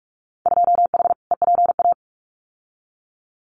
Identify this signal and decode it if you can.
Morse code, heard as a keyed tone, about 44 wpm, 700 Hz. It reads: 2H EPA
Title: Winter Field Day Exchanges Encore 45wpm